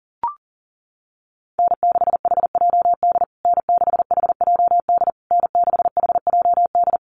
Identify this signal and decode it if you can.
Morse code, heard as a keyed tone, about 40 wpm, 700 Hz. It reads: N6HJD N6HJD N6HJD
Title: USA Call Signs Repeated 3x 40wpm